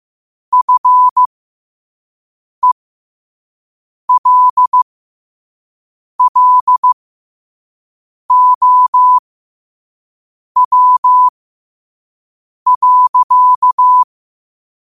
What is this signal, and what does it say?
Morse code, heard as a keyed tone, about 15 wpm, spaced as for 5 wpm, 1000 Hz. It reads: FELLOW.
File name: qotdfiles / 5wpm